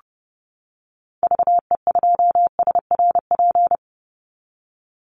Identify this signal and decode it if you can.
Morse code, heard as a keyed tone, about 30 wpm, 700 Hz. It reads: VE2SRP
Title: Call Signs - 30wpm